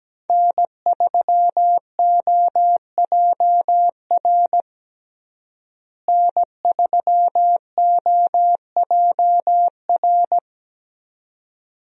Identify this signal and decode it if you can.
Morse code, heard as a keyed tone, about 17 wpm, 700 Hz. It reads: N3OJR N3OJR
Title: USA Call Signs - 5 Characters - Repeated 2x - 17wpm